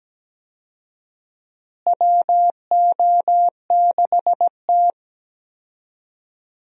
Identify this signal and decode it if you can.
Morse code, heard as a keyed tone, about 17 wpm, 700 Hz. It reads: WO6T
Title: USA Call Signs - 4 Characters - Repeated 3x - 17wpm